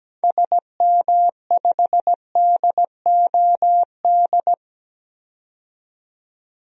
Morse code, heard as a keyed tone, about 17 wpm, 700 Hz. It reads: SM5DOD
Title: DXCC-100 Call Signs - 6 Characters - 17wpm